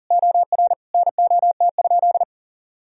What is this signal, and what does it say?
Morse code, heard as a keyed tone, about 40 wpm, 700 Hz. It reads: OR NOT?
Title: Sentences from Top 1000 Words - Review Part 3 - 40wpm